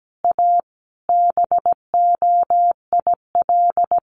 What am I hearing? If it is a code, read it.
Morse code, heard as a keyed tone, about 17 wpm, 700 Hz. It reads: A BOIL
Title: Sets of 5 Words Encore 17wpm